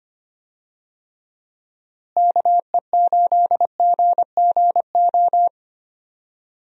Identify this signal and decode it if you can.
Morse code, heard as a keyed tone, about 25 wpm, 700 Hz. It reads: KE8GGO
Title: USA Call Signs - 6 Characters - Repeated 2x - 25wpm